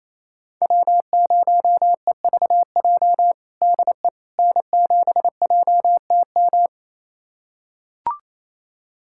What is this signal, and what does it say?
Morse code, heard as a keyed tone, about 28 wpm, 700 Hz. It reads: W0EVJ DE N7JTM
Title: USA Callsign DE Callsign - 5 Characters - 28wpm